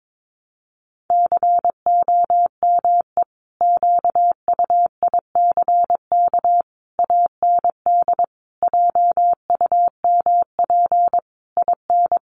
Morse code, heard as a keyed tone, about 22 wpm, 700 Hz. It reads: COME QUICK AND JUMP IN
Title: Sentences from Top 1300 Words 22wpm